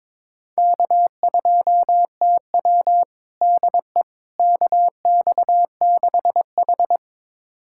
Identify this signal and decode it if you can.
Morse code, heard as a keyed tone, about 21 wpm, 700 Hz. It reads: K2TW DE KX6H
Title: USA Callsign DE Callsign - 4 Characters - 22wpm